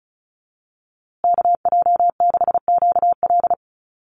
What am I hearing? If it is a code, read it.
Morse code, heard as a keyed tone, about 34 wpm, 700 Hz. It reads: KJ6QL
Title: USA Call Signs Repeated 3x 35wpm